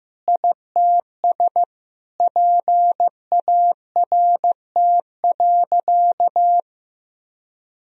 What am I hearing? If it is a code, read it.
Morse code, heard as a keyed tone, about 15 wpm, 700 Hz. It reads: ITS PART.
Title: Sentences from Top 1000 Words - Review Part 3 - 15wpm